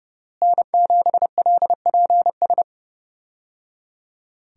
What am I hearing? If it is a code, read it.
Morse code, heard as a keyed tone, about 31 wpm, 700 Hz. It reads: N7LPS
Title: USA Call Signs - 5 Characters - Repeated 3x - 30wpm